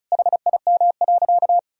Morse code, heard as a keyed tone, about 34 wpm, 700 Hz. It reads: HIM.
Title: Sentences from Top 1000 Words - Review Part 1 - 35wpm